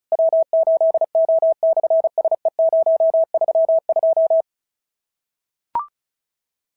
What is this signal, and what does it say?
Morse code, heard as a keyed tone, about 33 wpm, 650 Hz. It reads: W8O/SE032
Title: SOTA QSO Practice 35wpm